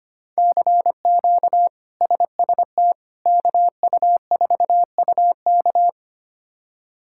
Morse code, heard as a keyed tone, about 25 wpm, 700 Hz. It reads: CQ SST KU4UK
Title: K1USN SST QSO Practice - 25wpm